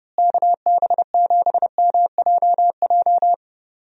Morse code, heard as a keyed tone, about 31 wpm, 700 Hz. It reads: KB7MJJ
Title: USA Call Signs - 6 Characters - Repeated 3x - 30wpm